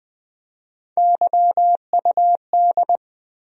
Morse code, heard as a keyed tone, about 20 wpm, 700 Hz. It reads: YUD
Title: USA Call Sign Postfixes - 3 Letters 20wpm